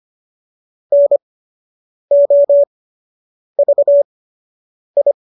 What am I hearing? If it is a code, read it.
Morse code, heard as a keyed tone, about 25 wpm, spaced as for 8 wpm, 570 Hz. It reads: NOVI